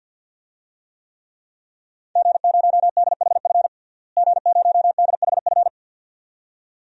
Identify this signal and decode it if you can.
Morse code, heard as a keyed tone, about 48 wpm, 700 Hz. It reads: G0DHF G0DHF
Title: DXCC-100 Call Signs - 5 Characters - Repeated 2x - 50wpm